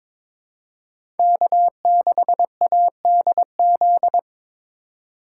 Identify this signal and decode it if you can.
Morse code, heard as a keyed tone, about 22 wpm, 700 Hz. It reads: K6ADZ
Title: USA Call Signs - 5 Characters - 2x Character Spacing - 22wpm